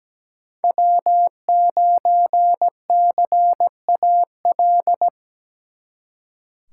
Morse code, heard as a keyed tone, about 20 wpm, 700 Hz. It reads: W9CAL
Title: Speed Racing - USA Call Signs 17wpm